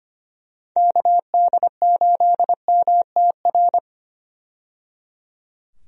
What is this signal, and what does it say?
Morse code, heard as a keyed tone, about 25 wpm, 700 Hz. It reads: KD8MTR
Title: USA Call Signs - 6 Characters - 25wpm